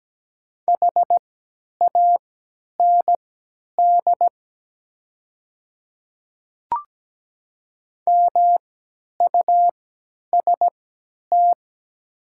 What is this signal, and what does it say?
Morse code, heard as a keyed tone, about 17 wpm, spaced as for 10 wpm, 700 Hz. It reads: HAND MUST
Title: 3x Character Spacing - Top 200 Words - 17wpm